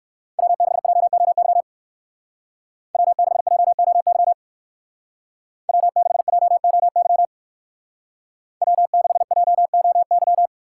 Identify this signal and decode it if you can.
Morse code, heard as a keyed tone, about 55 wpm, 700 Hz. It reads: W6JOY W6JOY W6JOY W6JOY
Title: Speed Racing - USA 1x3 Call Signs 45wpm